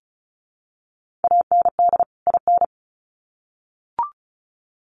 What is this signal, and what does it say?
Morse code, heard as a keyed tone, about 34 wpm, 700 Hz. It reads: AND IN
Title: Sets of 2 Words 35wpm